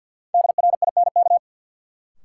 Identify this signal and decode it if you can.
Morse code, heard as a keyed tone, about 53 wpm, 700 Hz. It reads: DRINK
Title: Sentences from Top 1500 Words - New Word Drill - 50wpm